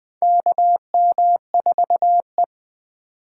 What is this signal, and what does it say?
Morse code, heard as a keyed tone, about 20 wpm, 700 Hz. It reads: KM4E
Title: USA Call Signs - 4 Characters - 2x Character Spacing - 20wpm